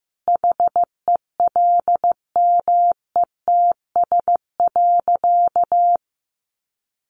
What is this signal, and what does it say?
Morse code, heard as a keyed tone, about 15 wpm, 700 Hz. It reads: HELMETS.
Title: A Sound Of Thunder 15wpm